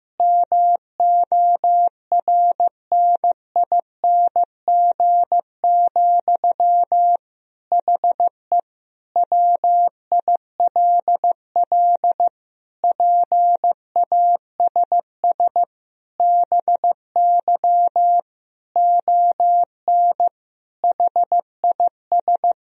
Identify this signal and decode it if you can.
Morse code, heard as a keyed tone, about 16 wpm, 700 Hz. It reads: MORNING, HE WILL PASS BY ON HIS
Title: Sentences from Top 1000 Words - Review Part 1 - 15wpm